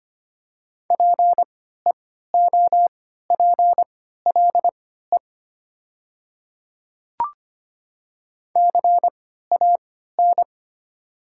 Morse code, heard as a keyed tone, about 25 wpm, spaced as for 14 wpm, 700 Hz. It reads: PEOPLE CAN
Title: 3x Character Spacing - Top 100 Words - 25wpm